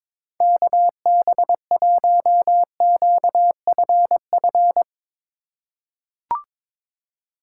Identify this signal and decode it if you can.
Morse code, heard as a keyed tone, about 22 wpm, 700 Hz. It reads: KB1QFF
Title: USA Call Signs - 6 Characters - 3x Character Spacing - 22wpm